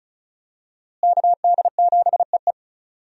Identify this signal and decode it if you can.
Morse code, heard as a keyed tone, about 36 wpm, 700 Hz. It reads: KD7EE
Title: USA Call Signs - 5 Characters - Repeated 2x - 35wpm